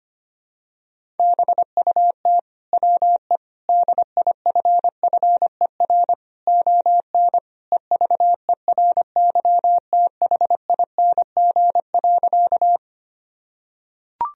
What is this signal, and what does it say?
Morse code, heard as a keyed tone, about 25 wpm, 700 Hz. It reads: BUT WE DIFFER ON EVERYTHING.
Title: Sentences from Top 1000 Words - Review Part 1 - 25wpm